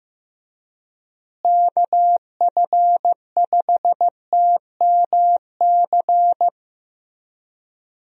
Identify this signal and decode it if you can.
Morse code, heard as a keyed tone, about 15 wpm, 700 Hz. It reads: KF5TMC
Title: USA Call Signs - 6 Characters - Repeated 3x - 15wpm